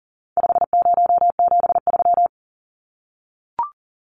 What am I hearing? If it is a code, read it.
Morse code, heard as a keyed tone, about 40 wpm, 700 Hz. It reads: 5073
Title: Numbers - 4-Digit 40wpm